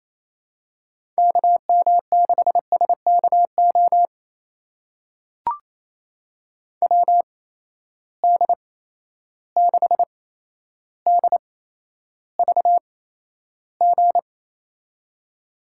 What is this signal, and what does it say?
Morse code, heard as a keyed tone, about 28 wpm, spaced as for 12 wpm, 700 Hz. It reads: KM6SKO WD6DVG
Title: USA Call Signs - 6 Characters - 8x Character Spacing - 28wpm